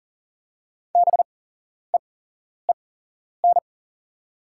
Morse code, heard as a keyed tone, about 44 wpm, spaced as for 11 wpm, 690 Hz. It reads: BEEN
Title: 8x Character Spacing - Top 300 Words - 40wpm